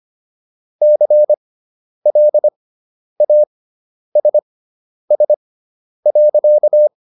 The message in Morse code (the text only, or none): CLASS.